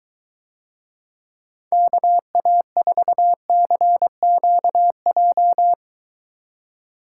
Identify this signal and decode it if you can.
Morse code, heard as a keyed tone, about 23 wpm, 700 Hz. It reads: KA4CQJ